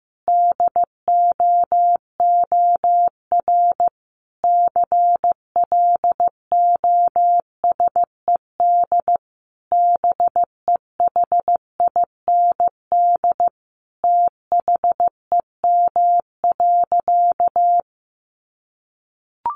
Morse code, heard as a keyed tone, about 15 wpm, 700 Hz. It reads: DOOR CLOSED BEHIND THEM.
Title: Sentences from Top 1400 Words 15wpm